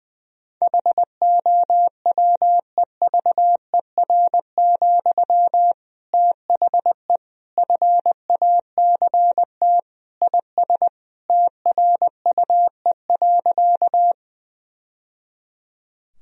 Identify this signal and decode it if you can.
Morse code, heard as a keyed tone, about 20 wpm, 700 Hz. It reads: HOWEVER, THE FACT IS TRUE.
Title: Sentences from Top 1000 Words - Review Part 3 - 20wpm